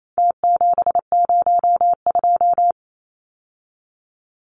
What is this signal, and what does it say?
Morse code, heard as a keyed tone, about 28 wpm, 700 Hz. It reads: T702
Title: Numbers - Contest Style - 4-Digit 28wpm